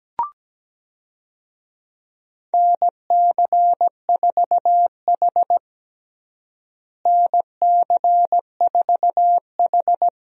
Morse code, heard as a keyed tone, about 17 wpm, 700 Hz. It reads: NC4H NC4H
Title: USA Call Signs - 4 Characters - Repeated 3x - 17wpm